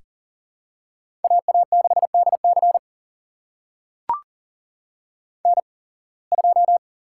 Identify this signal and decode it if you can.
Morse code, heard as a keyed tone, about 38 wpm, spaced as for 26 wpm, 700 Hz. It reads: AA6DC N2
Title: USA Call Signs - 5 Characters - 8x Character Spacing - 40wpm